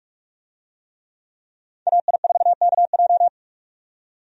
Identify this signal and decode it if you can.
Morse code, heard as a keyed tone, about 42 wpm, 700 Hz. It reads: AI4KJ
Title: USA Call Signs - 5 Characters - 45wpm